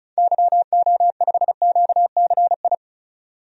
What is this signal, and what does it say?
Morse code, heard as a keyed tone, about 36 wpm, 700 Hz. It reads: YO5QCI
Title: DXCC-100 Call Signs - 6 Characters - 35wpm